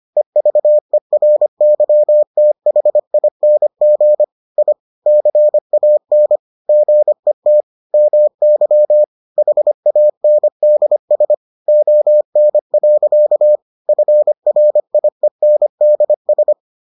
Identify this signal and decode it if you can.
Morse code, heard as a keyed tone, about 25 wpm, 600 Hz. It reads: EVERYTHING I CAN GET MY HANDS ON. FRIENDS